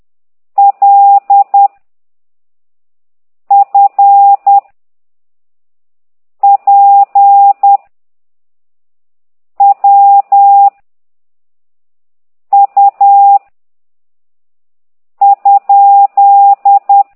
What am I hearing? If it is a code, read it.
Morse code, heard as a keyed tone, about 10 wpm, 800 Hz. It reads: L F P W U ?
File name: ct2idw / exerc8